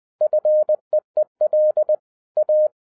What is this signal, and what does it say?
Morse code, heard as a keyed tone, about 20 wpm, 610 Hz. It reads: FEEL A